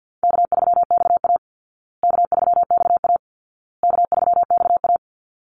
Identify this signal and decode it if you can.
Morse code, heard as a keyed tone, about 50 wpm, 700 Hz. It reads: K3XA K3XA K3XA
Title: Call Signs Repeated 2x - 50wpm